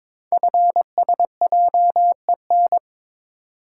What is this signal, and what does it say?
Morse code, heard as a keyed tone, about 22 wpm, 700 Hz. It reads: FSJEN
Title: Code Groups - 5 Letters - 22wpm